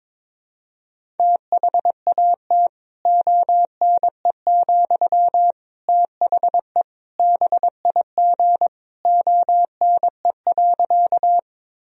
Morse code, heard as a keyed tone, about 22 wpm, 700 Hz. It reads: THAT ONE, THE BIG ONE.
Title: Sentences from Top 1000 Words - Review Part 1 - 22wpm